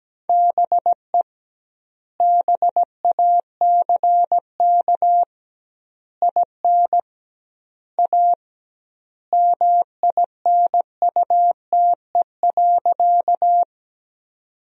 Morse code, heard as a keyed tone, about 17 wpm, 700 Hz. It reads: BE BACK IN A MINUTE.
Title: 2x Word-Spacing - Sentences from Top 500 Words - 17wpm